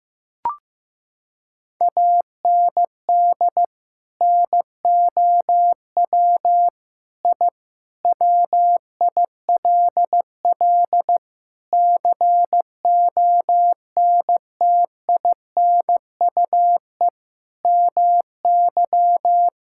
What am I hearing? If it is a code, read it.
Morse code, heard as a keyed tone, about 15 wpm, 700 Hz. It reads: AND NOW I WILL CONTINUE MY
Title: Sentences from Top 1000 Words - Review Part 6 - 15wpm